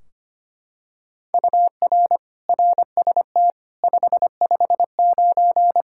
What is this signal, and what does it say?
Morse code, heard as a keyed tone, about 25 wpm, 700 Hz. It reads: UR RST 559
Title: Practice QSO - Exchange 1 - Beginning - 25wpm